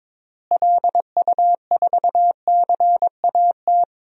Morse code, heard as a keyed tone, about 22 wpm, 700 Hz. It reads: LU4CAT
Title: DXCC-100 Call Signs - 6 Characters - Repeated 2x - 22wpm